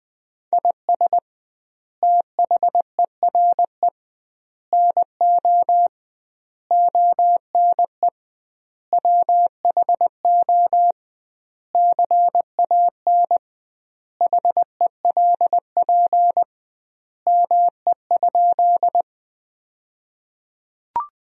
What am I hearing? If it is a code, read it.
Morse code, heard as a keyed tone, about 20 wpm, 700 Hz. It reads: IS THERE NO ONE WHO CAN HELP ME?
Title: 2x Word-Spacing - Sentences from Top 200 Words - 20wpm